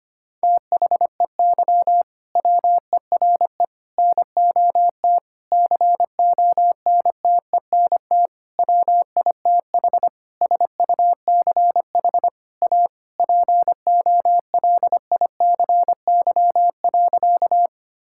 Morse code, heard as a keyed tone, about 25 wpm, 700 Hz. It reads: THEY WERE NOT CONTENT WITH SUCH A POLICY.
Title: Sentences from Top 1000 Words - Review Part 6 - 25wpm